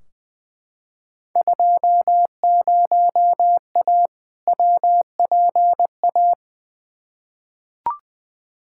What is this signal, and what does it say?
Morse code, heard as a keyed tone, about 21 wpm, 700 Hz. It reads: 20A WPA
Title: Field Day Exchanges Encore - 20wpm